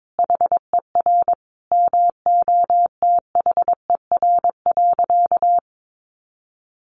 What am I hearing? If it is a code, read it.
Morse code, heard as a keyed tone, about 22 wpm, 700 Hz. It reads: HER MOTHER.
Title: Sentences from Top 1300 Words 22wpm